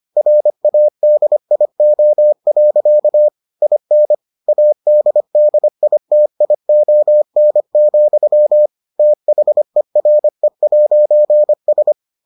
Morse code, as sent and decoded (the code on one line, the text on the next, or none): .-. .- -.. .. --- .-.-.- / .. -. / .- -.. -.. .. - .. --- -. --..-- / - .... .-. .----. ...
RADIO. IN ADDITION, THERE'S